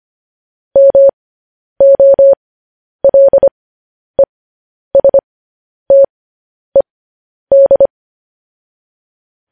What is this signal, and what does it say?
Morse code, heard as a keyed tone, about 25 wpm, spaced as for 10 wpm, 550 Hz. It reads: MOLESTED